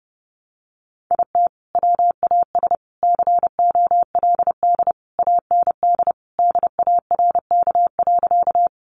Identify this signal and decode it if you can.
Morse code, heard as a keyed tone, about 30 wpm, 700 Hz. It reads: IT WAS COLD AND DARK.
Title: Sentences from Top 500 Words 30wpm